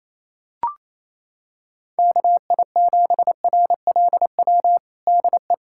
Morse code, heard as a keyed tone, about 27 wpm, 700 Hz. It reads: KI7RLW DE
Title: USA Callsign DE Callsign - 6 Characters - 28wpm